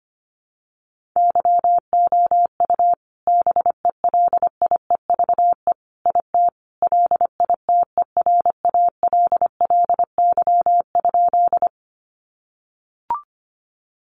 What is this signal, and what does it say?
Morse code, heard as a keyed tone, about 25 wpm, 700 Hz. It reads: YOU BELIEVE IT LITERALLY?
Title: Sentences from Top 1400 Words 25wpm